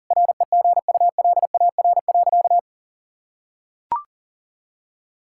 Morse code, heard as a keyed tone, about 37 wpm, 700 Hz. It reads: REGULAR.
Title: Sentences from Top 1000 Words - Review Part 6 - 40wpm